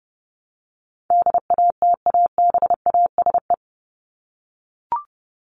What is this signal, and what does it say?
Morse code, heard as a keyed tone, about 30 wpm, 700 Hz. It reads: DATABASE
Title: Top 1751-2000 Words 30wpm